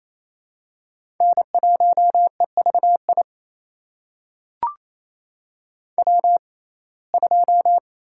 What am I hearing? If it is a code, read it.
Morse code, heard as a keyed tone, about 27 wpm, spaced as for 20 wpm, 700 Hz. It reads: N1EVI W2
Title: USA Call Signs - 5 Characters - 6x Character Spacing - 28wpm